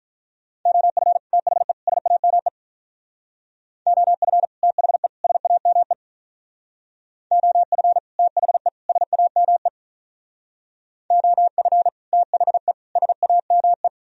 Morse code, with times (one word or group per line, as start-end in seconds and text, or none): 0.7-1.2 OF
1.3-1.7 THE
1.9-2.5 SAME
3.9-4.5 OF
4.6-5.1 THE
5.2-5.9 SAME
7.3-8.0 OF
8.2-8.7 THE
8.9-9.7 SAME
11.1-11.9 OF
12.1-12.7 THE
13.0-13.9 SAME